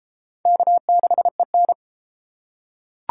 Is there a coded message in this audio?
Yes: Morse code from 0.5 to 1.7 s, heard as a keyed tone, about 32 wpm, 700 Hz. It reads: K6EN